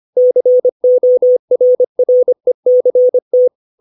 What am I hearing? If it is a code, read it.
Morse code, heard as a keyed tone, about 25 wpm, 500 Hz. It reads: CORRECT
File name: 632 words / 632 Words 25WPM